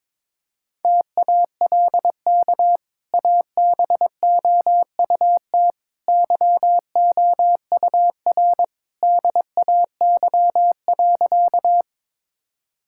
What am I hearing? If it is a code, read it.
Morse code, heard as a keyed tone, about 22 wpm, 700 Hz. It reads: TALK ABOUT YOUR DAY.
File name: practice / 22wpm / Easy Sentences with Hints - 22wpm